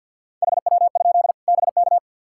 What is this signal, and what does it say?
Morse code, heard as a keyed tone, about 52 wpm, 700 Hz. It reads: HW? BK
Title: K1USN SST QSO Practice 50wpm